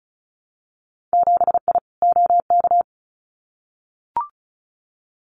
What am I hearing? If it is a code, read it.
Morse code, heard as a keyed tone, about 34 wpm, 700 Hz. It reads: 7I OK
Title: Winter Field Day Exchanges Encore 35wpm